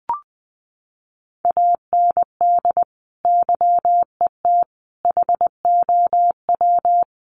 Morse code, heard as a keyed tone, about 20 wpm, 700 Hz. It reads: AND YET HOW
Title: Sentences from Top 1300 Words 20wpm